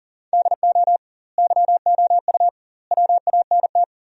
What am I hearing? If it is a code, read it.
Morse code, heard as a keyed tone, about 37 wpm, 700 Hz. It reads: DO YOU WANT
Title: 2x Word-Spacing - Sentences from Top 200 Words - 40wpm